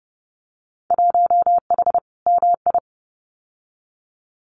Morse code, heard as a keyed tone, about 30 wpm, 700 Hz. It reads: 1H MI
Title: Winter Field Day Exchanges 30wpm